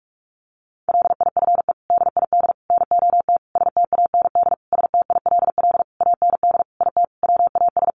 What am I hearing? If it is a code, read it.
Morse code, heard as a keyed tone, about 44 wpm, 700 Hz. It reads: LIFE DID NOT STAND STILL AND IT WAS
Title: Sentences from Top 800 Words 45wpm